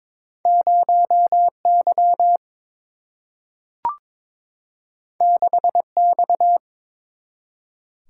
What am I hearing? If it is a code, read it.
Morse code, heard as a keyed tone, about 22 wpm, 700 Hz. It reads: 0Y 6X
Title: Single-Letter Single-Number 22wpm